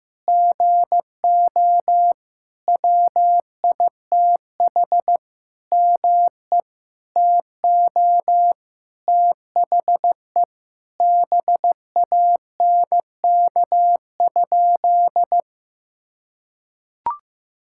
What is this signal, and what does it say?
Morse code, heard as a keyed tone, about 16 wpm, 700 Hz. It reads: GO WITH ME TO THE BANK?
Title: Sentences from Top 1000 Words - Review Part 6 - 15wpm